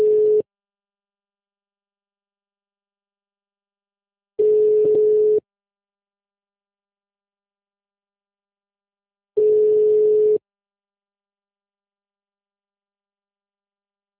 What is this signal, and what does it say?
Ringback tone (Europe, 425 Hz).